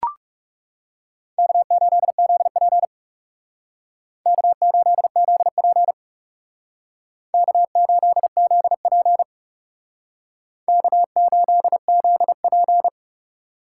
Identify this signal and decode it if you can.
Morse code, heard as a keyed tone, about 38 wpm, 700 Hz. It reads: K8ZP K8ZP K8ZP K8ZP